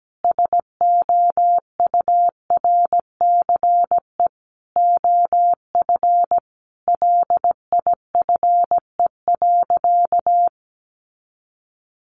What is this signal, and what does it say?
Morse code, heard as a keyed tone, about 17 wpm, 700 Hz. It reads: SOURCE OF LIFE.
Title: Sentences from Top 800 Words 17wpm